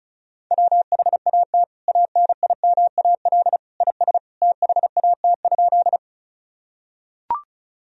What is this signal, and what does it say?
Morse code, heard as a keyed tone, about 34 wpm, 700 Hz. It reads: WHAT ANIMAL IS THAT?
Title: Sentences from Top 1000 Words - Review Part 1 - 35wpm